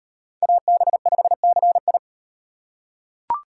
Morse code, heard as a keyed tone, about 34 wpm, 700 Hz. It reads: AB5CI